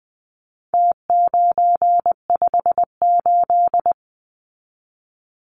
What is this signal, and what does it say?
Morse code, heard as a keyed tone, about 20 wpm, 700 Hz. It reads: T958
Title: Numbers - Contest Style - 4-Digit 20wpm